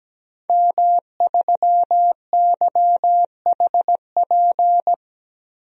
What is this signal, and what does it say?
Morse code, heard as a keyed tone, about 17 wpm, 700 Hz. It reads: M3YHP
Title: DXCC-100 Call Signs - 5 Characters - Repeated 2x - 17wpm